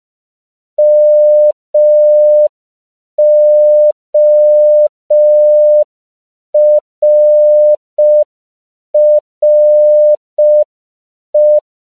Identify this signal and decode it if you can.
Morse code, heard as a keyed tone, about 5 wpm, 600 Hz. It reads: MORRE
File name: 5PPM